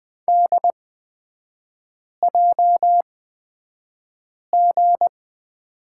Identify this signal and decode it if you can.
Morse code, heard as a keyed tone, about 20 wpm, spaced as for 5 wpm, 700 Hz. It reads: DJG
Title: 243 Lesson 39e - Letter z - Three Characters 20wpm